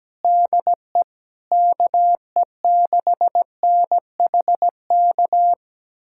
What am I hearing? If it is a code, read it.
Morse code, heard as a keyed tone, about 17 wpm, 700 Hz. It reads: DE KE6NHK
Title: USA Callsign DE Callsign 17wpm